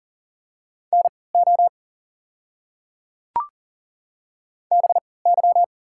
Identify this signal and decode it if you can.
Morse code, heard as a keyed tone, about 38 wpm, spaced as for 22 wpm, 700 Hz. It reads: NO BY
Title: 3x Character Spacing - Top 100 Words - 40wpm